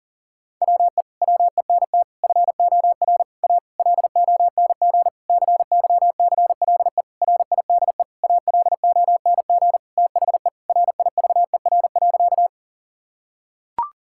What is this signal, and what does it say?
Morse code, heard as a keyed tone, about 40 wpm, 700 Hz. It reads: WE WENT FOR A LONG CYCLE RIDE ALONG THE RIVER.
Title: Sentences from Top 1000 Words - Review Part 6 - 40wpm